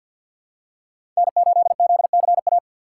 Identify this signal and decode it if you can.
Morse code, heard as a keyed tone, about 47 wpm, 700 Hz. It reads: N8ZCA
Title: USA Call Signs - 5 Characters - 3x Character Spacing - 50wpm